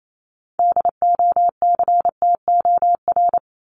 Morse code, heard as a keyed tone, about 28 wpm, 700 Hz. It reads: DOCTOR
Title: Top 1251-1500 Words 28wpm